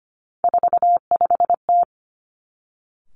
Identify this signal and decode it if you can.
Morse code, heard as a keyed tone, about 25 wpm, 700 Hz. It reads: 45T
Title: Three Challenging Letter-Number Combinations 25wpm